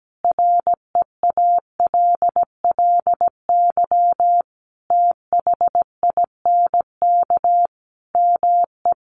Morse code, heard as a keyed tone, about 17 wpm, 700 Hz. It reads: REALLY THINK ME
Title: Sentences from Top 1200 Words 17wpm